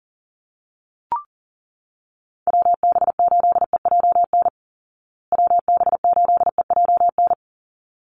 Morse code, heard as a keyed tone, about 40 wpm, 700 Hz. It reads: WB8EJN WB8EJN
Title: Call Signs Repeated 3x - 40wpm